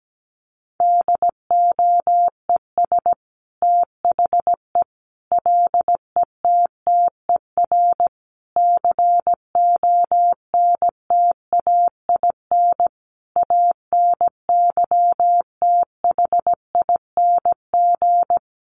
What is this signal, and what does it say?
Morse code, heard as a keyed tone, about 17 wpm, 700 Hz. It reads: DOES THE LETTER CONTAIN ANYTHING
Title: Sentences from Top 1100 Words 17wpm